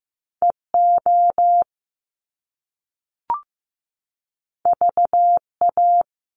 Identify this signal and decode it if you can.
Morse code, heard as a keyed tone, about 15 wpm, 700 Hz. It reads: EO VA
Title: Top Two Letter Combinations in Words 15wpm